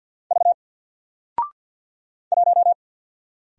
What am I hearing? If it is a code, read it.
Morse code, heard as a keyed tone, about 48 wpm, 700 Hz. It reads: V 1
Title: Single Character (Letter:Number:Punctuation:Prosign) Alt 50wpm